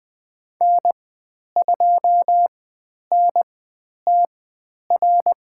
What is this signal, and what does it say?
Morse code, heard as a keyed tone, about 20 wpm, spaced as for 10 wpm, 700 Hz. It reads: N2NTR
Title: 102 Lesson 17g - Letter u - Call Signs 20wpm